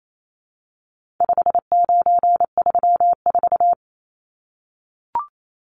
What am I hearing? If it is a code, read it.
Morse code, heard as a keyed tone, about 28 wpm, 700 Hz. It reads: 5934
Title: Numbers - 4-Digit 28wpm